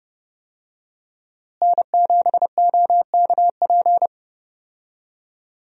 Morse code, heard as a keyed tone, about 30 wpm, 700 Hz. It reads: N7OKP
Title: USA Call Signs - 5 Characters - Repeated 2x - 30wpm